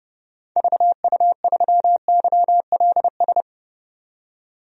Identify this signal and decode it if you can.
Morse code, heard as a keyed tone, about 30 wpm, 700 Hz. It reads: VU3YLS
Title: DXCC-100 Call Signs - 6 Characters - 30wpm